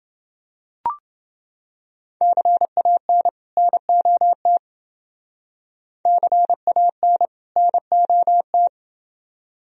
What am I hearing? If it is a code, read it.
Morse code, heard as a keyed tone, about 28 wpm, 700 Hz. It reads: CAN NOT CAN NOT